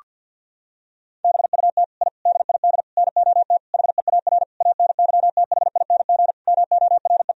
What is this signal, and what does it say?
Morse code, heard as a keyed tone, about 50 wpm, 700 Hz. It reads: BUT I DID NOT HEAR ANYTHING MORE
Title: Sentences from Top 1000 Words - Review Part 4 - 50wpm